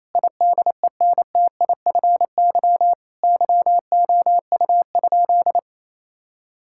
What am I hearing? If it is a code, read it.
Morse code, heard as a keyed tone, about 28 wpm, 700 Hz. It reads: IDENTIFY YOU?
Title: Sentences from Top 1000 Words - Review Part 6 - 28wpm